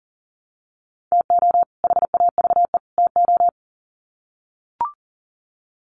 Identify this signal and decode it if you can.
Morse code, heard as a keyed tone, about 40 wpm, 700 Hz. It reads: TO HAVE TO
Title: Sets of 3 Words 40wpm